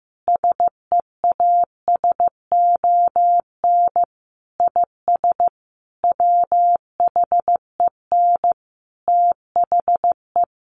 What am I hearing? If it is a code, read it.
Morse code, heard as a keyed tone, about 15 wpm, 700 Hz. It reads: SEASON IS WHEN THE
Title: Sentences from Top 1400 Words 15wpm